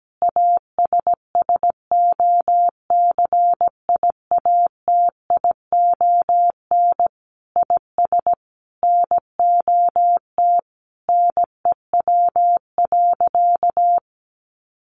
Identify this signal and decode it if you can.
Morse code, heard as a keyed tone, about 17 wpm, 700 Hz. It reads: ASSOCIATION IS NOT NEW.
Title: Sentences from Top 1100 Words 17wpm